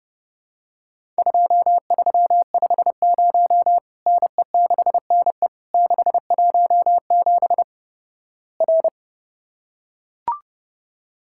Morse code, heard as a keyed tone, about 29 wpm, 700 Hz. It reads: 2350 NE6NE 617 R
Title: Worked All Europe DX Contest - QTC Exchange - 30wpm